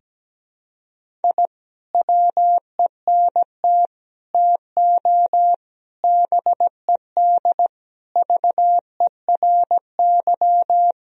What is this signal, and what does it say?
Morse code, heard as a keyed tone, about 17 wpm, 700 Hz. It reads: I WENT TO BED VERY
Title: Sentences from Top 1000 Words - Review Part 2 - 17wpm